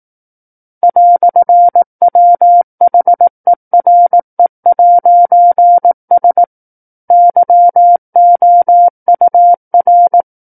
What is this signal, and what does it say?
Morse code, heard as a keyed tone, about 18 wpm, 700 Hz. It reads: "WHERE'S YOUR
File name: 18 WPM